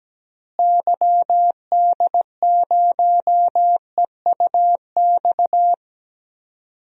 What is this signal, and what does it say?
Morse code, heard as a keyed tone, about 17 wpm, 700 Hz. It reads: YD0EUX
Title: DXCC-100 Call Signs - 6 Characters - Repeated 3x - 17wpm